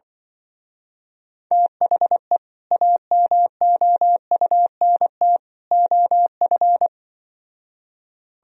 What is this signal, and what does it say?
Morse code, heard as a keyed tone, about 24 wpm, 700 Hz. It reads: THE AMOUNT OF
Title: Speed-Racing - Sets of 3 Words 20wpm